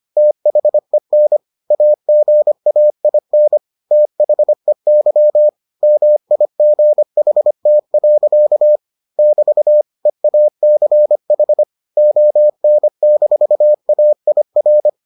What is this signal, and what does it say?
Morse code, heard as a keyed tone, about 25 wpm, 600 Hz. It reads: THEN AGAIN THEY MIGHT. <BT> EACH ON-AIR